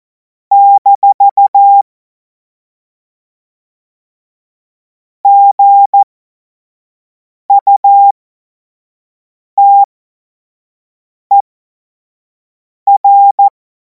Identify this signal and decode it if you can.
Morse code, heard as a keyed tone, about 14 wpm, spaced as for 5 wpm, 800 Hz. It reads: - GUTER
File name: Morsekurs / Witze / Witz 23. Juni 2026 25 F70